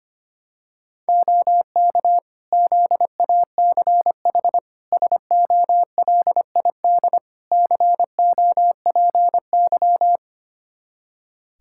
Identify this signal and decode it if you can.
Morse code, heard as a keyed tone, about 25 wpm, 700 Hz. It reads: OK ZACH SOLID COPY
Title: K1USN SST QSO Practice 25wpm